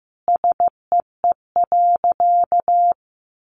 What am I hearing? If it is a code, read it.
Morse code, heard as a keyed tone, about 15 wpm, 700 Hz. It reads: SEE.
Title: Sentences from Top 500 Words 15wpm